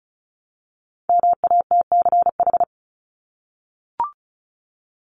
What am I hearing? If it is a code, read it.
Morse code, heard as a keyed tone, about 34 wpm, 700 Hz. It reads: MATCH
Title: Top 1501-1750 Words 35wpm